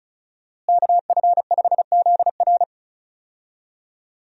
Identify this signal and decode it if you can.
Morse code, heard as a keyed tone, about 36 wpm, 700 Hz. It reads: KF5ZR
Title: USA Call Signs - 5 Characters - Repeated 2x - 35wpm